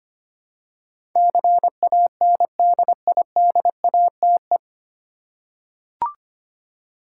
Morse code, heard as a keyed tone, about 25 wpm, 700 Hz. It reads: CANDIDATE
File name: practice / 25wpm / 9 Letter Words - First 3 Letters Given - 25wpm